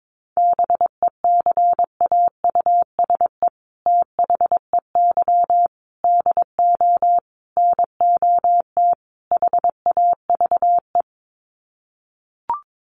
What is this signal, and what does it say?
Morse code, heard as a keyed tone, about 22 wpm, 700 Hz. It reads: BECAUSE THEY DO NOT HAVE
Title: Sets of 5 Words Encore2 22wpm